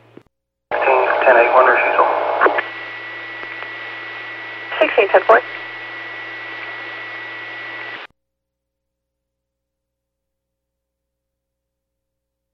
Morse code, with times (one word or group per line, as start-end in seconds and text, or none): none